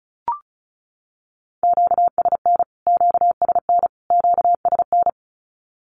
Morse code, heard as a keyed tone, about 34 wpm, 700 Hz. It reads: QSN QSN QSN
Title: Q-Codes Repeated 3x 35wpm